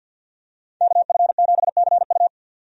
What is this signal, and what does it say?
Morse code, heard as a keyed tone, about 52 wpm, 700 Hz. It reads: KF7CU
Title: USA Call Signs - 5 Characters - 50wpm